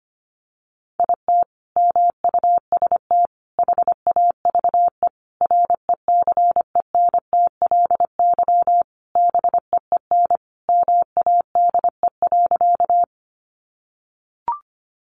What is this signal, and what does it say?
Morse code, heard as a keyed tone, about 25 wpm, 700 Hz. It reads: IT MUST HAVE RECENTLY BEEN MADE.
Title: Sentences from Top 1200 Words 25wpm